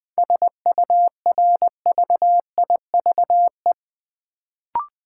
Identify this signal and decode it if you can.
Morse code, heard as a keyed tone, about 20 wpm, 700 Hz. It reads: SURVIVE
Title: Sentences from Top 1300 Words - New Word Drill - 20wpm